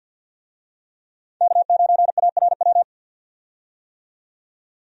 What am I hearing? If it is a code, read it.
Morse code, heard as a keyed tone, about 52 wpm, 700 Hz. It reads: K9ARW
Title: USA Call Signs - 5 Characters - 50wpm